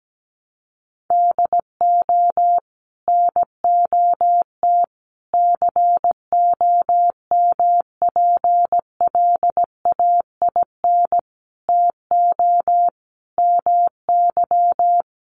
Morse code, heard as a keyed tone, about 17 wpm, 700 Hz. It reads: DO NOT COMPLAIN TO MY
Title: Sentences from Top 1300 Words 17wpm